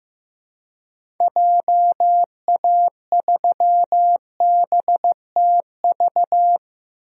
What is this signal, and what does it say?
Morse code, heard as a keyed tone, about 15 wpm, 700 Hz. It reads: JA3BTV
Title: DXCC-100 Call Signs - 6 Characters - Repeated 3x - 15wpm